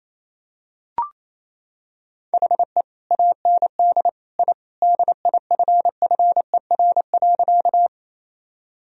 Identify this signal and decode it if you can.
Morse code, heard as a keyed tone, about 28 wpm, 700 Hz. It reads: HE AND I DIFFER.
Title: Sentences from Top 1000 Words - Review Part 1 - 28wpm